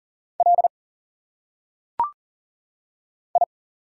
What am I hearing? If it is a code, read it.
Morse code, heard as a keyed tone, about 40 wpm, 700 Hz. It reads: L I
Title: Single Letters - Phonetic Pronunciation 40wpm